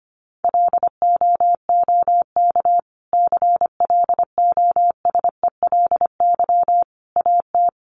Morse code, heard as a keyed tone, about 25 wpm, 700 Hz. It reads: LOOK CLOSELY AT
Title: Sentences from Top 1400 Words 25wpm